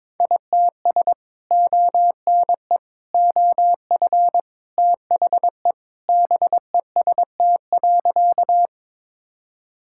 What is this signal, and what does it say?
Morse code, heard as a keyed tone, about 22 wpm, 700 Hz. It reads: ITS ONE OF THE BEST.
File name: practice / 22wpm / Sentences with 3-Word Phrases - 22wpm